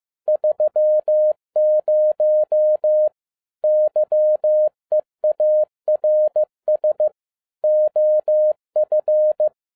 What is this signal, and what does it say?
Morse code, heard as a keyed tone, about 15 wpm, 610 Hz. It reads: 30 YEARS OF